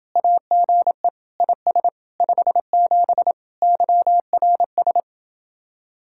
Morse code, heard as a keyed tone, about 28 wpm, 700 Hz. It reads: AGE IS 57 YRS